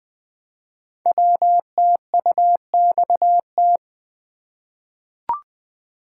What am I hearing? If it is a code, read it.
Morse code, heard as a keyed tone, about 20 wpm, 700 Hz. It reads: WTUXT
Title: Code Groups - 5 Letters - 20wpm